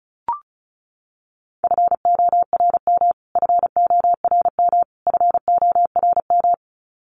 Morse code, heard as a keyed tone, about 34 wpm, 700 Hz. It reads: FORM FORM FORM
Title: Top 200 Words Repeated 35wpm